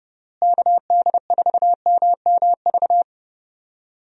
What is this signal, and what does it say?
Morse code, heard as a keyed tone, about 30 wpm, 700 Hz. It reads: KD4MMV
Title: USA Call Signs - 6 Characters - Repeated 3x - 30wpm